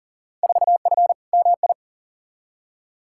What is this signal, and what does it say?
Morse code, heard as a keyed tone, about 40 wpm, 700 Hz. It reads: 4F MI
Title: Field Day Exchanges - 40wpm